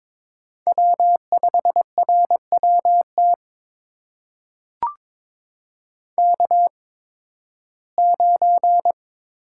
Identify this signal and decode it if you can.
Morse code, heard as a keyed tone, about 22 wpm, spaced as for 14 wpm, 700 Hz. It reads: W5RWT K9
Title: USA Call Signs - 5 Characters - 8x Character Spacing - 22wpm